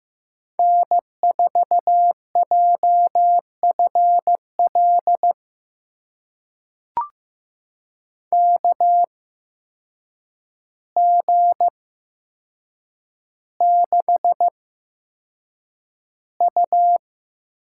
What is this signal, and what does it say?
Morse code, heard as a keyed tone, about 15 wpm, spaced as for 7 wpm, 700 Hz. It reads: N4JFL KG6U
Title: USA Call Signs - 5 Characters - 8x Character Spacing - 15wpm